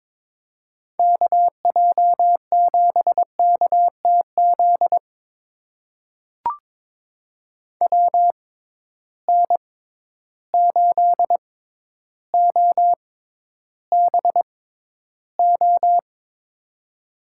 Morse code, heard as a keyed tone, about 22 wpm, spaced as for 11 wpm, 700 Hz. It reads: KJ7KTZ WN8OBO
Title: USA Call Signs - 6 Characters - 6x Character Spacing - 22wpm